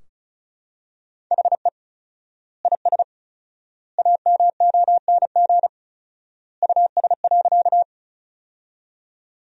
Morse code, heard as a keyed tone, about 36 wpm, 700 Hz. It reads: HE IS AMONG US.
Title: 4x Word-Spacing - Sentences from Top 500 Words - 35wpm